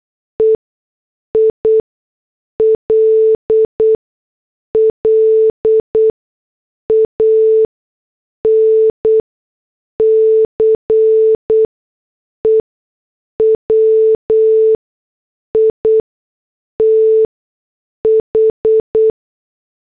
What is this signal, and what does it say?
Morse code, heard as a keyed tone, about 8 wpm, spaced as for 6 wpm, 440 Hz. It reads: EILLANCE WITH